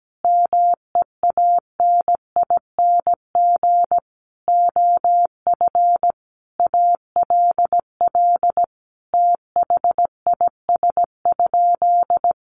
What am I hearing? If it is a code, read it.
Morse code, heard as a keyed tone, about 17 wpm, 700 Hz. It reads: MEANING OF ALL THIS?
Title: Sentences from Top 900 Words 17wpm